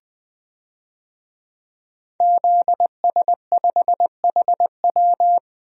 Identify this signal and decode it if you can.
Morse code, heard as a keyed tone, about 20 wpm, 700 Hz. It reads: ZS5HW
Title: DXCC-100 Call Signs - 5 Characters - Repeated 3x - 20wpm